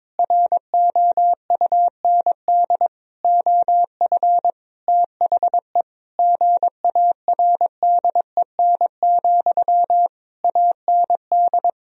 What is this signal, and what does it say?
Morse code, heard as a keyed tone, about 22 wpm, 700 Hz. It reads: ROUND OF THE GARDEN, AND
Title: Sentences from Top 1000 Words - Review Part 4 - 22wpm